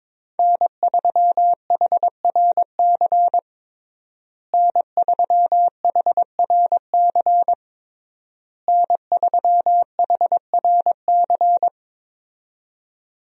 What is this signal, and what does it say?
Morse code, heard as a keyed tone, about 22 wpm, 700 Hz. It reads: N3HRC N3HRC N3HRC
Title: USA Call Signs - 5 Characters - Repeated 3x - 22wpm